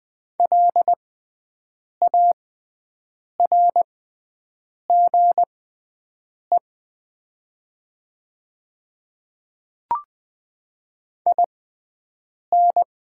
Morse code, heard as a keyed tone, about 20 wpm, spaced as for 7 wpm, 700 Hz. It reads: LARGE IN
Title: 6x Character Spacing - Top 200 Words - 20wpm